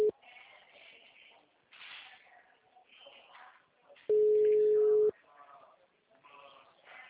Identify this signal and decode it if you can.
Ringback tone (Europe, 425 Hz), with speech.